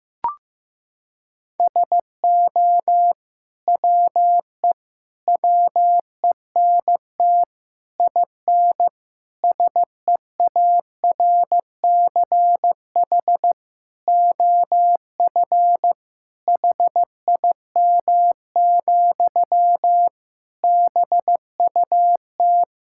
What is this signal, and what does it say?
Morse code, heard as a keyed tone, about 15 wpm, 700 Hz. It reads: SO WE WENT IN SEARCH OF HIM, BUT
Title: Sentences from Top 1000 Words - Review Part 5 - 15wpm